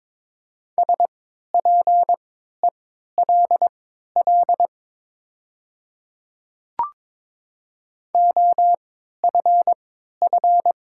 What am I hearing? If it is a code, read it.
Morse code, heard as a keyed tone, about 22 wpm, spaced as for 13 wpm, 700 Hz. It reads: SPELL OFF
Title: 3x Character Spacing - Top 500 Words - 22wpm